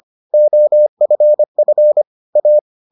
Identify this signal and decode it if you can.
Morse code, heard as a keyed tone, about 25 wpm, 600 Hz. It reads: OFF A